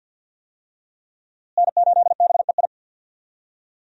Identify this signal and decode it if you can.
Morse code, heard as a keyed tone, about 50 wpm, 700 Hz. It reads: N8BEI
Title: USA Call Signs - 5 Characters - Repeated 3x - 50wpm